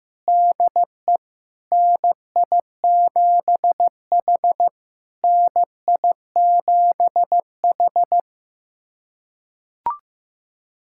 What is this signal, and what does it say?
Morse code, heard as a keyed tone, about 15 wpm, 700 Hz. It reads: DE NI7H NI7H
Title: SOTA QSO Practice 15wpm